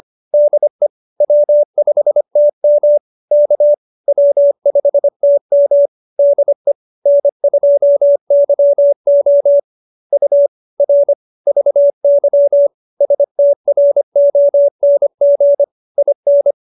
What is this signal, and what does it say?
Morse code, heard as a keyed tone, about 25 wpm, 570 Hz. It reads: DE W5TM K W5TM DE N2YO U R VY STRONG IN